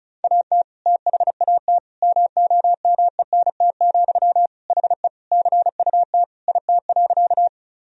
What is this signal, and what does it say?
Morse code, heard as a keyed tone, about 34 wpm, 700 Hz. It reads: AT THAT MOMENT, HE CUT IT.
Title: Sentences from Top 1000 Words - Review Part 1 - 35wpm